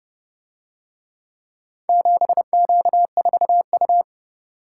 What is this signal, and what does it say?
Morse code, heard as a keyed tone, about 29 wpm, 700 Hz. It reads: 7Q4U